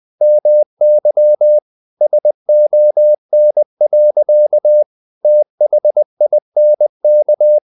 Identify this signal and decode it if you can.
Morse code, heard as a keyed tone, about 20 wpm, 600 Hz. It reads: MY SON. THINK
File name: the old man and the sea / the old man and the sea part 1 20wpm